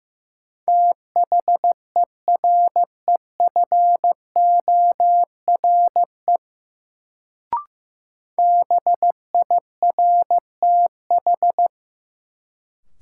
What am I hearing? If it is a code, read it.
Morse code, heard as a keyed tone, about 15 wpm, 700 Hz. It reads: THEREFORE BIRTH
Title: Sentences from Top 800 Words - New Word Drill - 15wpm